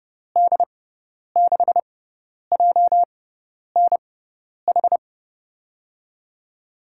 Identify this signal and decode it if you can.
Morse code, heard as a keyed tone, about 30 wpm, spaced as for 10 wpm, 700 Hz. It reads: D6JNH
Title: USA Call Signs - 6x Character Spacing - 30wpm